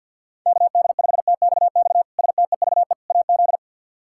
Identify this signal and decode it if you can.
Morse code, heard as a keyed tone, about 52 wpm, 700 Hz. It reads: KD5TXX STEVE AZ
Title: K1USN SST QSO Practice - 50wpm